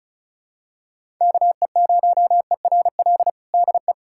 Morse code, heard as a keyed tone, about 36 wpm, 700 Hz. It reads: KE0ERL DE
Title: USA Callsign DE Callsign 35wpm